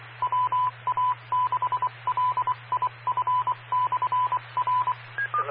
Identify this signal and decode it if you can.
Morse code, heard as a keyed tone, about 24 wpm, 1000 Hz, with radio noise. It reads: WA6LIF/R
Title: Claremont, CA Police, Amateur Radio